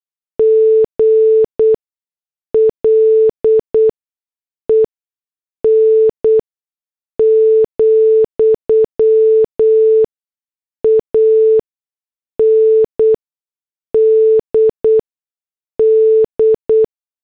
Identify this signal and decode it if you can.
Morse code, heard as a keyed tone, about 8 wpm, spaced as for 6 wpm, 440 Hz. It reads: GLEN, AND D